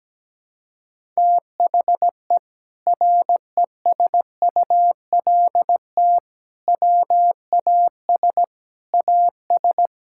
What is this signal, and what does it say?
Morse code, heard as a keyed tone, about 17 wpm, 700 Hz. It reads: THE RESULT WAS AS
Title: Sentences from Top 1000 Words - Review Part 5 - 17wpm